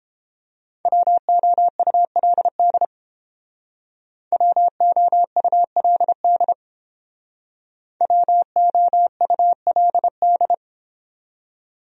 Morse code, heard as a keyed tone, about 29 wpm, 700 Hz. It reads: WOULD WOULD WOULD